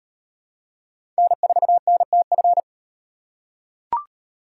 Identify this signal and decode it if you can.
Morse code, heard as a keyed tone, about 35 wpm, 700 Hz. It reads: N4NTF